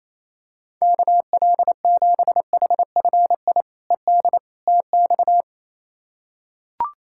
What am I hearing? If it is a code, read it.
Morse code, heard as a keyed tone, about 27 wpm, 700 Hz. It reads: KL7HFI ED TX
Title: K1USN SST QSO Practice - 28wpm